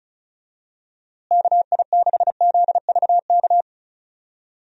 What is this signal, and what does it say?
Morse code, heard as a keyed tone, about 36 wpm, 700 Hz. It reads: KI6ZVK
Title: USA Call Signs - 6 Characters - Repeated 3x - 35wpm